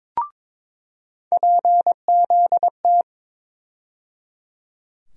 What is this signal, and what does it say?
Morse code, heard as a keyed tone, about 21 wpm, 700 Hz. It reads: PZT
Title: USA Call Sign Postfixes - 3 Letters 22wpm